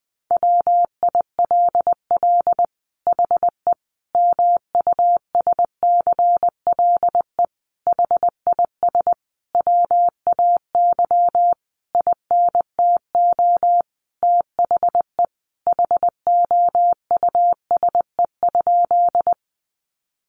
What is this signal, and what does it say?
Morse code, heard as a keyed tone, about 20 wpm, 700 Hz. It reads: WILL HE MUSCLE HIS WAY INTO THE HOUSE?
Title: Sentences from Top 1300 Words 20wpm